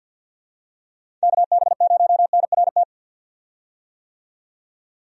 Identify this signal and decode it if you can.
Morse code, heard as a keyed tone, about 51 wpm, 700 Hz. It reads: KB0NRT
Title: USA Call Signs - 6 Characters - 50wpm